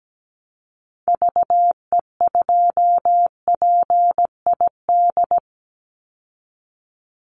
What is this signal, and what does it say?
Morse code, heard as a keyed tone, about 17 wpm, 700 Hz. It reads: VE2PID
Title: Call Signs Repeated 3x - 17wpm